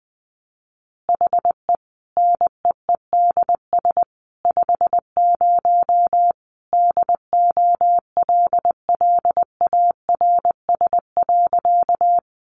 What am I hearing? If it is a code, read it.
Morse code, heard as a keyed tone, about 20 wpm, 700 Hz. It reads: HE NEEDS 50 DOLLARS.
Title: Additional Easy Phrases 20wpm